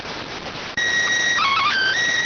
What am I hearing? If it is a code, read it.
SSTV (slow-scan television), PD 290.